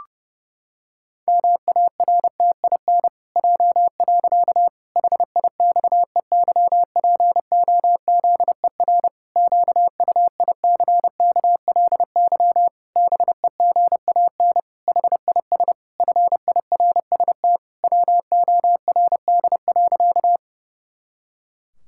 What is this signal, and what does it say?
Morse code, heard as a keyed tone, about 30 wpm, 700 Hz. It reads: MARTIN J. HIXEYPOZER QUICKLY BEGAN HIS FIRST WORD.
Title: Pangram Challenge v2 - 30wpm